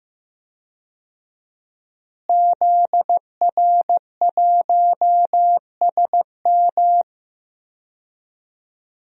Morse code, heard as a keyed tone, about 16 wpm, 700 Hz. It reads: ZR1SM
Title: DXCC-100 Call Signs - 5 Characters - Repeated 3x - 15wpm